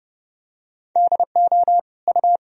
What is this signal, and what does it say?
Morse code, heard as a keyed tone, about 32 wpm, 700 Hz. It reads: DO U
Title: Sentences from Top 100 Words and Abbreviations 30wpm